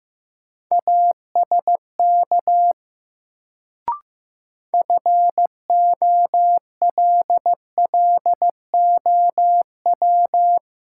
Morse code, heard as a keyed tone, about 15 wpm, 700 Hz. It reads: ASK FOLLOW
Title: Sentences from Top 200 Words - New Word Drill - 15wpm